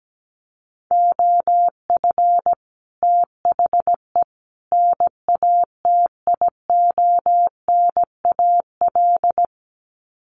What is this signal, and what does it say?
Morse code, heard as a keyed tone, about 17 wpm, 700 Hz. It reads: OF THE NATIONAL
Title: Sets of 3 Words 17wpm